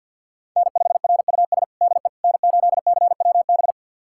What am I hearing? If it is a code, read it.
Morse code, heard as a keyed tone, about 49 wpm, 700 Hz. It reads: N5RUS DE N8CWB
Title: USA Callsign DE Callsign - 5 Characters - 50wpm